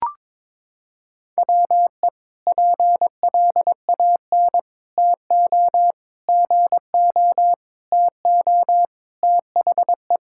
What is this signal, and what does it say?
Morse code, heard as a keyed tone, about 22 wpm, 700 Hz. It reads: WE PLAN TO GO TO THE
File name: practice / 22wpm / Sentences with 4-Word Phrases - 22wpm